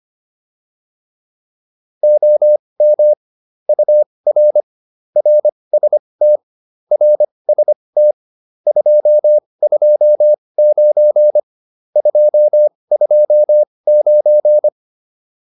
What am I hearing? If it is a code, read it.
Morse code, heard as a keyed tone, about 25 wpm, spaced as for 20 wpm, 600 Hz. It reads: OM UR RST RST 229 229
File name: QSO402 20